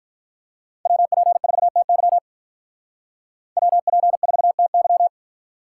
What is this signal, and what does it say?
Morse code, heard as a keyed tone, about 44 wpm, 700 Hz. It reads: WP4TY WP4TY